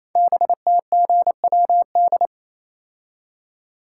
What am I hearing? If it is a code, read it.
Morse code, heard as a keyed tone, about 27 wpm, 700 Hz. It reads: BTGWD
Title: Code Groups - 5 Letters - 28wpm